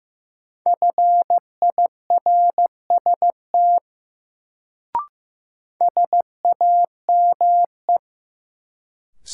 Morse code, heard as a keyed tone, about 16 wpm, 700 Hz. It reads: FIRST SAME
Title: Sets of 2 Words - New Word Drill - 15wpm